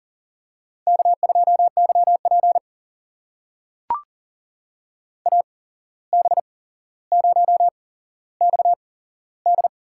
Morse code, heard as a keyed tone, about 40 wpm, spaced as for 16 wpm, 700 Hz. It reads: K2YP AB0XD
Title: USA Call Signs - 8x Character Spacing - 40wpm